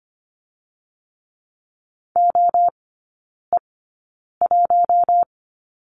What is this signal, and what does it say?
Morse code, heard as a keyed tone, about 25 wpm, spaced as for 9 wpm, 700 Hz. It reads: OE1
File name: Lesson 17e - Letter u - Three Characters 25wpm